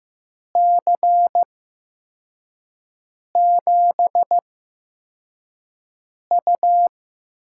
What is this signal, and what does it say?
Morse code, heard as a keyed tone, about 15 wpm, spaced as for 4 wpm, 700 Hz. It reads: C7U
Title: USA Call Signs - 6 Characters - 8x Character Spacing - 15wpm